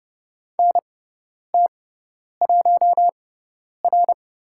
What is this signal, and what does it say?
Morse code, heard as a keyed tone, about 30 wpm, spaced as for 10 wpm, 700 Hz. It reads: NT1R
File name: Lesson 16g - Letter c - Call Signs 30wpm